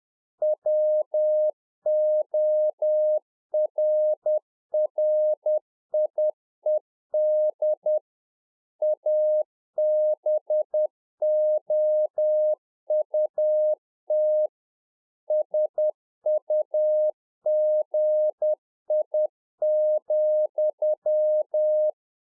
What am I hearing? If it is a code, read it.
Morse code, heard as a keyed tone, about 10 wpm, 610 Hz. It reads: WORRIED ABOUT SUGI,